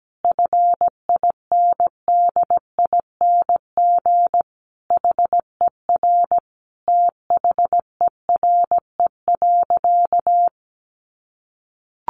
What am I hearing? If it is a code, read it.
Morse code, heard as a keyed tone, about 17 wpm, 700 Hz. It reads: FINDING HER THERE.
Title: Sentences from Top 1500 Words 17wpm